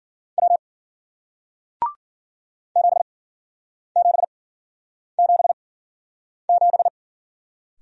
Morse code, heard as a keyed tone, about 54 wpm, spaced as for 9 wpm, 700 Hz. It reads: W 7777